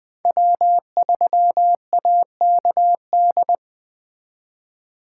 Morse code, heard as a keyed tone, about 23 wpm, 700 Hz. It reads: W3AKD
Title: Speed Racing - USA 1x3 Call Signs 20wpm